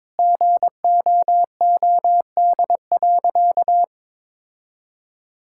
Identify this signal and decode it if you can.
Morse code, heard as a keyed tone, about 22 wpm, 700 Hz. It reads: GOOD.
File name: practice / 22wpm / Sentences with 3-Word Phrases - 22wpm